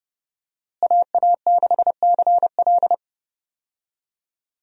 Morse code, heard as a keyed tone, about 30 wpm, 700 Hz. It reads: AA6CL
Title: USA Call Signs - 5 Characters - Repeated 2x - 30wpm